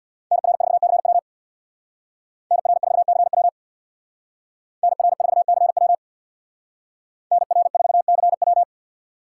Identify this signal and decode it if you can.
Morse code, heard as a keyed tone, about 61 wpm, 700 Hz. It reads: NR4CW NR4CW NR4CW NR4CW
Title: Speed Racing - USA 2x2 Call Signs 50wpm